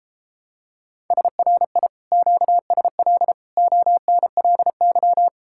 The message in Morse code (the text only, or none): SRI QSL ONLY